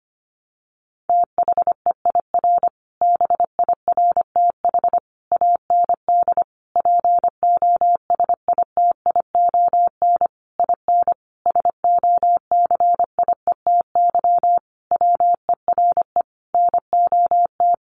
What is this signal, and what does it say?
Morse code, heard as a keyed tone, about 25 wpm, 700 Hz. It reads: THEIR BIRTH AND POSITION IN SOCIETY WERE NOT
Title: Sentences from Top 900 Words 25wpm